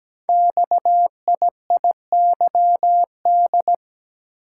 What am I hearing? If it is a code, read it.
Morse code, heard as a keyed tone, about 17 wpm, 700 Hz. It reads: XIIYD
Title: Code Groups - 5 Letter:Numbers - 17wpm